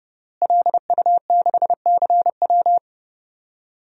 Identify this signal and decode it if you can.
Morse code, heard as a keyed tone, about 30 wpm, 700 Hz. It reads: LU6CW
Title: DXCC-100 Call Signs - 5 Characters - Repeated 2x - 30wpm